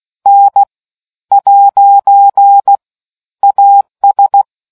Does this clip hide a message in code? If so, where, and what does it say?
Yes: Morse code from 0.3 to 4.4 s, heard as a keyed tone, about 16 wpm, spaced as for 10 wpm, 800 Hz. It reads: N'AS